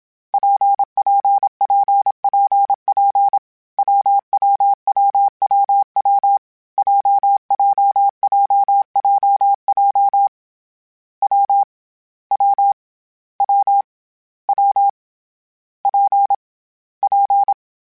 Morse code, heard as a keyed tone, about 27 wpm, 800 Hz. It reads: PPPPP WWWWW JJJJJ W W W W P P
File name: Variable Speed / Variable Speed-mp3s / WPJ WPJ1 sp @25